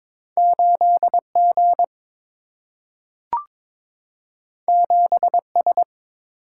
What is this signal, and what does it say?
Morse code, heard as a keyed tone, about 22 wpm, 700 Hz. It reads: 8G 7S
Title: Single-Letter Single-Number 22wpm